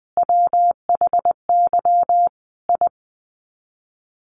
Morse code, heard as a keyed tone, about 20 wpm, 700 Hz. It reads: WHY I
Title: Sets of 5 Words Encore 20wpm